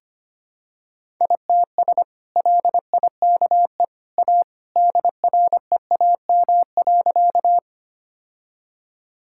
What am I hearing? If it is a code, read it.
Morse code, heard as a keyed tone, about 25 wpm, 690 Hz. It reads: ITS LIKE A DREAM.
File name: practice / 25wpm / Sentences with 2-Word Phrases - 25wpm